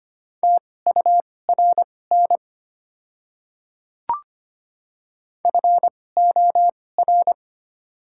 Morse code, heard as a keyed tone, about 25 wpm, spaced as for 18 wpm, 700 Hz. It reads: TURN FOR
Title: 2x Character Spacing - Top 200 Words - 25wpm